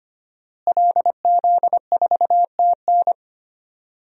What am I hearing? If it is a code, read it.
Morse code, heard as a keyed tone, about 25 wpm, 700 Hz. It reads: LZ4TN
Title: DXCC-100 Call Signs - 5 Characters - Repeated 3x - 25wpm